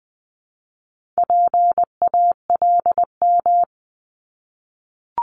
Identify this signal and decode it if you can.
Morse code, heard as a keyed tone, about 20 wpm, 700 Hz. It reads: PALM
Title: Multi-letter Phonemes Repeated 3x 20wpm